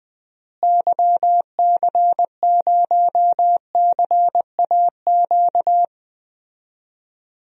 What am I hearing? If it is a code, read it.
Morse code, heard as a keyed tone, about 20 wpm, 700 Hz. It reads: YC0CAQ
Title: DXCC-100 Call Signs - 6 Characters - Repeated 2x - 20wpm